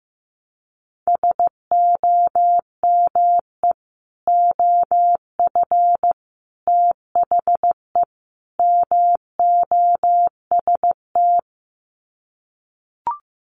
Morse code, heard as a keyed tone, about 15 wpm, 700 Hz. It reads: SOME OF THE MOST
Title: Sets of 4 Words 15wpm